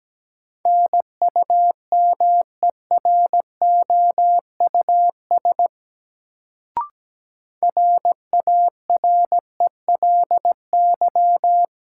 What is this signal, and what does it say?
Morse code, heard as a keyed tone, about 17 wpm, 700 Hz. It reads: NUMEROUS RARELY
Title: Sentences from Top 1400 Words - New Word Drill - 17wpm